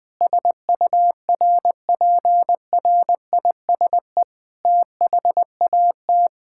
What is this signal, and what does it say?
Morse code, heard as a keyed tone, about 20 wpm, 700 Hz. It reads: SURPRISE THAT
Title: Sets of 8 Words - 20wpm